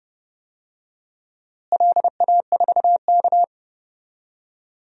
Morse code, heard as a keyed tone, about 30 wpm, 700 Hz. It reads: LA4K
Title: DXCC-100 Call Signs - 4 Characters - Repeated 2x - 30wpm